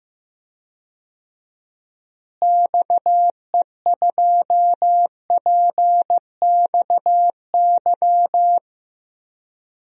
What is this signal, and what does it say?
Morse code, heard as a keyed tone, about 16 wpm, 700 Hz. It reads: XE2PXY
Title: DXCC-100 Call Signs - 6 Characters - Repeated 3x - 15wpm